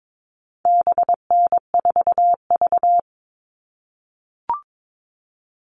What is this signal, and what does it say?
Morse code, heard as a keyed tone, about 22 wpm, 700 Hz. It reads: BN4V
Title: All Cut Numbers - 4-Digit 22wpm